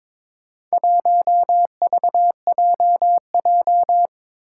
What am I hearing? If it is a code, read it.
Morse code, heard as a keyed tone, about 22 wpm, 700 Hz. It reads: 1VJJ
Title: Four Challenging Letter-Number Combinations 22wpm